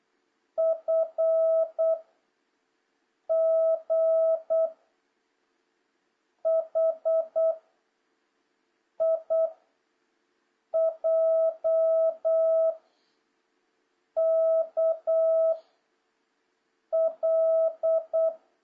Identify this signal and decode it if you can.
Morse code, heard as a keyed tone, about 8 wpm, 1300 Hz, with noise. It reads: F G H I J K L
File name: jh8cbh / 2026cw